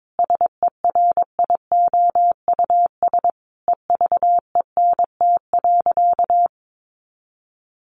Morse code, heard as a keyed tone, about 22 wpm, 700 Hz. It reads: SERIOUS EVENT.
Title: Sentences from Top 1400 Words 22wpm